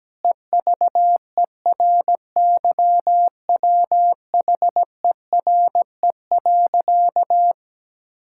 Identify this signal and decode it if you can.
Morse code, heard as a keyed tone, about 17 wpm, 700 Hz. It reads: EVERYWHERE.
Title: Sentences from Top 1600 Words 17wpm